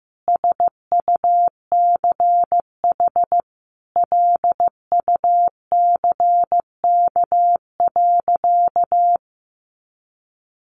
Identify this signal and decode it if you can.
Morse code, heard as a keyed tone, about 15 wpm, 700 Hz. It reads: SUCH LUCK.
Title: Sentences from Top 1500 Words 15wpm